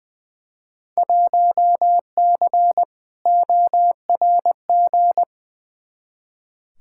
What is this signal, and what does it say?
Morse code, heard as a keyed tone, about 20 wpm, 700 Hz. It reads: 1C ORG
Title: Field Day Exchanges - 20wpm